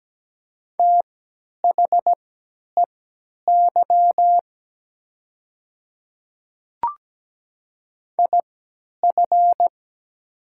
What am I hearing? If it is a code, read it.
Morse code, heard as a keyed tone, about 17 wpm, spaced as for 10 wpm, 700 Hz. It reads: THEY IF
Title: 3x Character Spacing - Top 100 Words - 17wpm